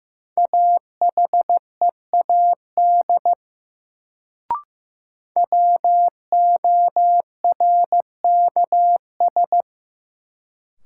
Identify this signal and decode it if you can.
Morse code, heard as a keyed tone, about 15 wpm, 700 Hz. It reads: AHEAD WORKS
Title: Sentences from Top 1100 Words - New Word Drill Alt - 15wpm